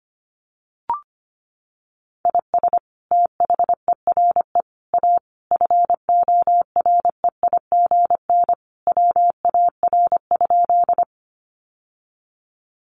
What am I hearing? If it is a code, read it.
Morse code, heard as a keyed tone, about 25 wpm, 700 Hz. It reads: IS THERE A FOREIGN WAR?
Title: Sentences from Top 1400 Words 25wpm